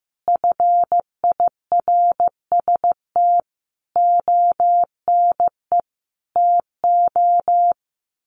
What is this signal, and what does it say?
Morse code, heard as a keyed tone, about 14 wpm, 700 Hz. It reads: FIRST ONE TO
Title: Sentences from Top 1300 Words 15wpm